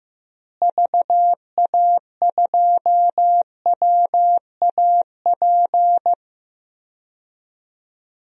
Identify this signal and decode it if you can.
Morse code, heard as a keyed tone, about 15 wpm, 700 Hz. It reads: VA2WAP
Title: DXCC-100 Call Signs - 6 Characters - Repeated 2x - 15wpm